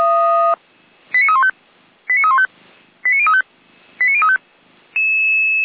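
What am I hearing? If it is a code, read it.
Five-tone selective call (ZVEI), heard as quick runs of five tones, tones of about 70 ms. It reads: 89315 89315 89035 89035
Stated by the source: ZVEI five-tone selective call received by SDR, logged by Feuerwehr Orlamünde.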